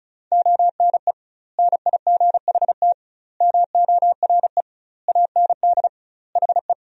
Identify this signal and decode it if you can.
Morse code, heard as a keyed tone, about 34 wpm, 700 Hz. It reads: ONE NIGHT MORE AND HE
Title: 2x Word-Spacing - Sentences from Top 300 Words - 35wpm